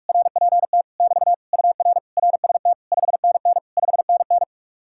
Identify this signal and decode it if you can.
Morse code, heard as a keyed tone, about 58 wpm, 700 Hz. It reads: RPT <BT> UR RST 5NN 5NN